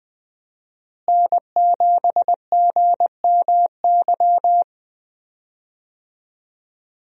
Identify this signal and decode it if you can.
Morse code, heard as a keyed tone, about 20 wpm, 700 Hz. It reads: N7GMY
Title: USA Call Signs - 5 Characters - Repeated 3x - 20wpm